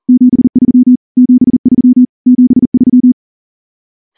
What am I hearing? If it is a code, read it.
Morse code, heard as a keyed tone, about 38 wpm, 250 Hz. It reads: 73 73 73
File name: morse / morse-training-40WPM